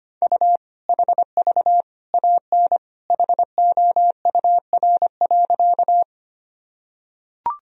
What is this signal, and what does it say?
Morse code, heard as a keyed tone, about 25 wpm, 700 Hz. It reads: U HV AN HOUR.
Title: Sentences from Top 400 Words and Abbreviations 25wpm